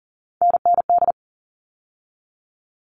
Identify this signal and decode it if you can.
Morse code, heard as a keyed tone, about 40 wpm, 700 Hz. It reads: NND
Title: All Cut Numbers - 3-Digit 40wpm